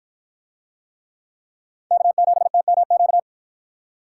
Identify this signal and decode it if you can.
Morse code, heard as a keyed tone, about 50 wpm, 700 Hz. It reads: K7TMQ